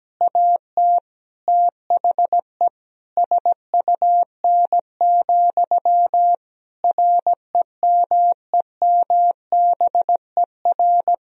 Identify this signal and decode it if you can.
Morse code, heard as a keyed tone, about 17 wpm, 700 Hz. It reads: AT THE SUN, REMEMBER
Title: Sentences from Top 1000 Words - Review Part 1 - 17wpm